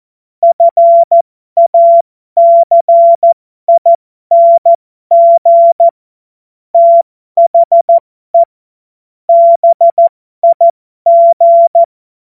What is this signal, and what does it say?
Morse code, heard as a keyed tone, about 14 wpm, spaced as for 12 wpm, 680 Hz. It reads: FACING THE BIG
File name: Morsekurs / Witze